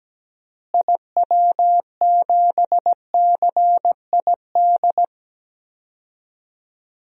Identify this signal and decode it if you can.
Morse code, heard as a keyed tone, about 17 wpm, 700 Hz. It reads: IW7CID
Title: DXCC-100 Call Signs - 6 Characters - Repeated 2x - 17wpm